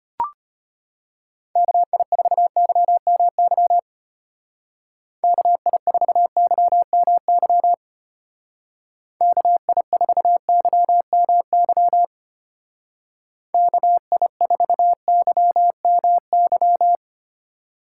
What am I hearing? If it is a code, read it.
Morse code, heard as a keyed tone, about 30 wpm, 700 Hz. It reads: KI4YMY KI4YMY KI4YMY KI4YMY
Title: Speed Racing - USA 2x3 Call Signs 25wpm